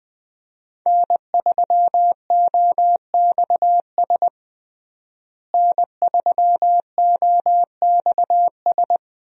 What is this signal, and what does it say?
Morse code, heard as a keyed tone, about 20 wpm, 700 Hz. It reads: N3OXS N3OXS
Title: USA Call Signs - 5 Characters - Repeated 2x - 20wpm